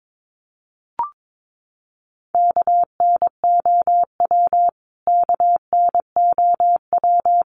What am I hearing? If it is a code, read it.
Morse code, heard as a keyed tone, about 22 wpm, 700 Hz. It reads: KNOW KNOW
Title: Top 300 Words Repeated 22wpm